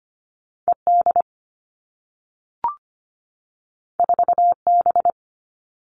Morse code, heard as a keyed tone, about 25 wpm, 700 Hz. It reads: E D 4 B
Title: All Cut Numbers - 2-Digit 25wpm